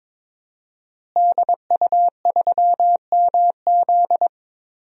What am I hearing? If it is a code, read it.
Morse code, heard as a keyed tone, about 21 wpm, 700 Hz. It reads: DU3MZ
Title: DXCC-100 Call Signs - 5 Characters - Repeated 2x - 22wpm